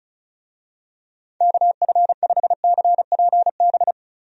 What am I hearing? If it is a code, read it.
Morse code, heard as a keyed tone, about 36 wpm, 700 Hz. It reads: KF5CPB
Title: USA Call Signs - 6 Characters - 35wpm